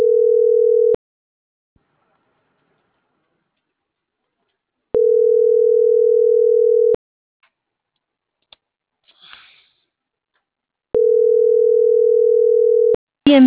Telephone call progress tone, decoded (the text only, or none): ringback tone